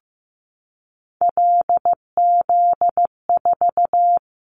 Morse code, heard as a keyed tone, about 15 wpm, 700 Hz. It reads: LZ4
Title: Callsign Prefixes for Top 25 DXCC - 15wpm